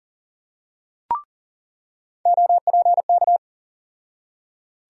Morse code, heard as a keyed tone, about 40 wpm, 700 Hz. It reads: OPK
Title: USA Call Sign Postfixes - 3 Letters 40wpm